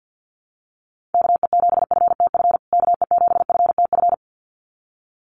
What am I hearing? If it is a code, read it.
Morse code, heard as a keyed tone, about 50 wpm, 700 Hz. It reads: KE7FTF KE7FTF
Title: USA Call Signs Repeated 2x 50wpm